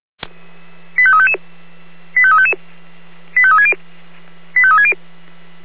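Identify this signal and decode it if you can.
Five-tone selective call (ZVEI), heard as quick runs of five tones, tones of about 70 ms. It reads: 85355 85355 85360 85360
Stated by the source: ZVEI five-tone selective call received on an SDR, from the Feuerwehr Orlamünde audio server.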